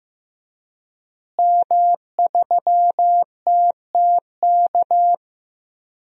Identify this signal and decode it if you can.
Morse code, heard as a keyed tone, about 15 wpm, 700 Hz. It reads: M3TTK
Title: DXCC-100 Call Signs - 5 Characters - Repeated 2x - 15wpm